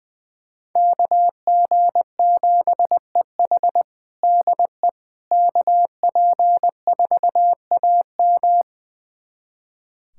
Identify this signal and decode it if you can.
Morse code, heard as a keyed tone, about 20 wpm, 700 Hz. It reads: KG7EH DE KP4AM
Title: USA Callsign DE Callsign - 5 Characters - 20wpm